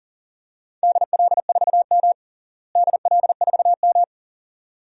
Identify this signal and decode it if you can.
Morse code, heard as a keyed tone, about 37 wpm, 700 Hz. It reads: DL4M DL4M
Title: DXCC-100 Call Signs - 4 Characters - Repeated 2x - 40wpm